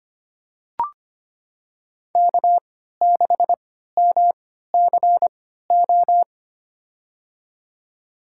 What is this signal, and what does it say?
Morse code, heard as a keyed tone, about 25 wpm, spaced as for 14 wpm, 700 Hz. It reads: K6MCO
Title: USA Call Signs - 5 Characters - 3x Character Spacing - 25wpm